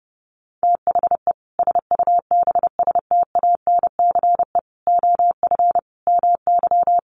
Morse code, heard as a keyed tone, about 28 wpm, 700 Hz. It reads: THE SUBSTANCE OF MY
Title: Sentences from Top 1500 Words 30wpm